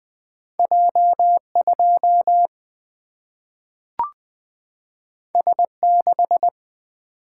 Morse code, heard as a keyed tone, about 20 wpm, 700 Hz. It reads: J2 S6
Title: Single-Letter Single-Number 20wpm